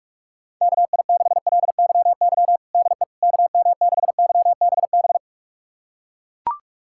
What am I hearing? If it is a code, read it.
Morse code, heard as a keyed tone, about 41 wpm, 700 Hz. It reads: KI6LYY DE KM6YBB
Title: USA Callsign DE Callsign - 6 Characters - 45wpm